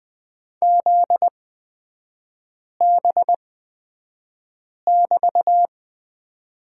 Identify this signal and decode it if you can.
Morse code, heard as a keyed tone, about 20 wpm, spaced as for 5 wpm, 700 Hz. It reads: ZB<BT>
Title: 243 Lesson 39e - Letter z - Three Characters 20wpm